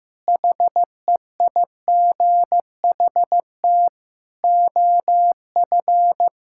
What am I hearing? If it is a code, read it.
Morse code, heard as a keyed tone, about 15 wpm, 700 Hz. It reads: HEIGHT OF
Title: Sentences from Top 1600 Words 15wpm